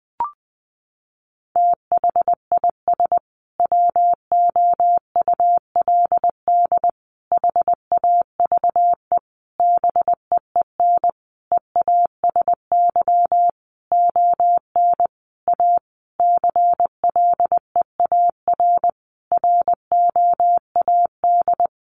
Morse code, heard as a keyed tone, about 20 wpm, 700 Hz. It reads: THIS WOULD HAVE BEEN EASY ON A CLEAR ROAD
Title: Sentences from Top 800 Words 20wpm